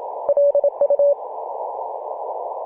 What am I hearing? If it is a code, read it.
Morse code, heard as a keyed tone, about 27 wpm, 570 Hz, with noise. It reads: LU
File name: Mar/26 2219z E51ANQ 18.078MHz CW